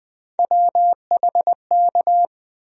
Morse code, heard as a keyed tone, about 21 wpm, 700 Hz. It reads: WHK
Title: USA Call Sign Postfixes - 3 Letters 20wpm